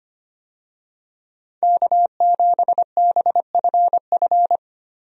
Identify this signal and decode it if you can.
Morse code, heard as a keyed tone, about 25 wpm, 700 Hz. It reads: K7BFF
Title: USA Call Signs - 5 Characters - 25wpm